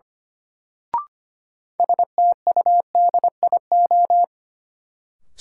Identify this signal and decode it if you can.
Morse code, heard as a keyed tone, about 25 wpm, 700 Hz. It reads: STUDIO
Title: Sentences from Top 1200 Words - New Word Drill Alt - 25wpm